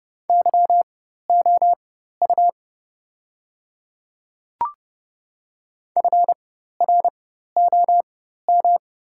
Morse code, heard as a keyed tone, about 30 wpm, spaced as for 14 wpm, 700 Hz. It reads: YOU FROM
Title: 4x Character Spacing - Top 100 Words - 30wpm